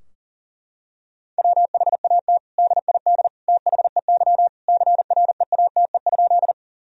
Morse code, heard as a keyed tone, about 37 wpm, 700 Hz. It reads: WHAT DID THEY CREATE?
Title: Sentences from Top 1000 Words - Review Part 3 - 40wpm